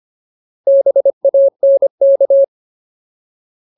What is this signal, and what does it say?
Morse code, heard as a keyed tone, about 25 wpm, 550 Hz. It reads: BANK
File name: Ultimate Word List / 25WPM Ultimate